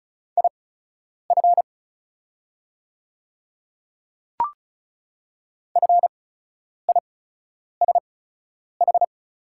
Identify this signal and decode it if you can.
Morse code, heard as a keyed tone, about 34 wpm, spaced as for 10 wpm, 700 Hz. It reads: IF FISH
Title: 8x Character Spacing - Top 300 Words - 35wpm